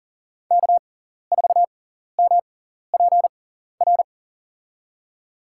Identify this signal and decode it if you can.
Morse code, heard as a keyed tone, about 40 wpm, spaced as for 14 wpm, 700 Hz. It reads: K4MPR
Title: USA Call Signs - 5 Characters - 6x Character Spacing - 40wpm